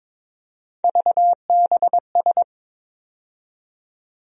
Morse code, heard as a keyed tone, about 22 wpm, 700 Hz. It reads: VBS
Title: USA Call Sign Postfixes - 3 Letters 22wpm